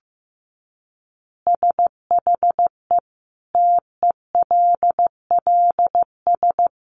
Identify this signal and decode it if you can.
Morse code, heard as a keyed tone, about 14 wpm, 700 Hz. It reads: SHE TELLS
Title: Additional Easy Phrases 15wpm